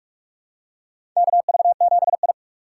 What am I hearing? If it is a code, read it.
Morse code, heard as a keyed tone, about 46 wpm, 700 Hz. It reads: KV7I
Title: USA Call Signs - 4 Characters - 3x Character Spacing - 45wpm